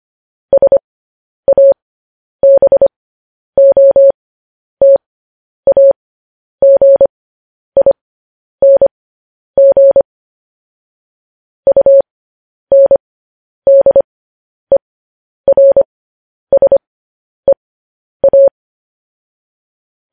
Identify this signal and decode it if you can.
Morse code, heard as a keyed tone, about 25 wpm, spaced as for 10 wpm, 550 Hz. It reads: SABOTAGING UNDERSEA
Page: News Headlines in Morse Code at 10 WPM